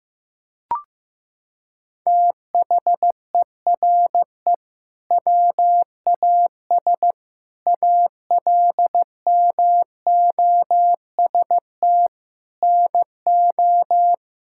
Morse code, heard as a keyed tone, about 15 wpm, 700 Hz. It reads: THERE WAS ALMOST NO
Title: Sentences from Top 1000 Words - Review Part 6 - 15wpm